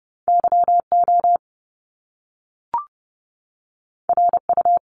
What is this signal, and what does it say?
Morse code, heard as a keyed tone, about 30 wpm, 700 Hz. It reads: YO RU
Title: Top Two Letter Combinations in Words 30wpm